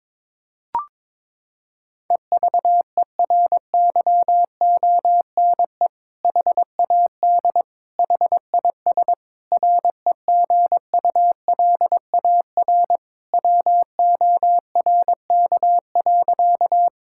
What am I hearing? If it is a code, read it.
Morse code, heard as a keyed tone, about 22 wpm, 700 Hz. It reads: EVERYONE HAD HIS REGULAR WORK.
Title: Sentences from Top 1000 Words - Review Part 6 - 22wpm